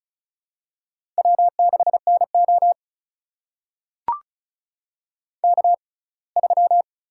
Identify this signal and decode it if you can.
Morse code, heard as a keyed tone, about 35 wpm, spaced as for 24 wpm, 700 Hz. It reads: W6NO K3
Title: USA Call Signs - 6x Character Spacing - 35wpm